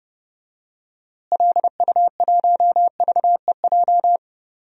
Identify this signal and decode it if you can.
Morse code, heard as a keyed tone, about 30 wpm, 700 Hz. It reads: LU1VEJ
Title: DXCC-100 Call Signs - 6 Characters - Repeated 3x - 30wpm